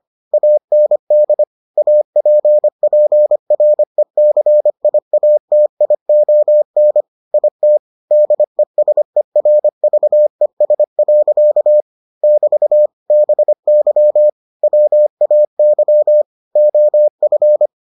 Morse code, heard as a keyed tone, about 25 wpm, 600 Hz. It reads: AND APPRECIATION IT DESERVES. <BT> BY WAY OF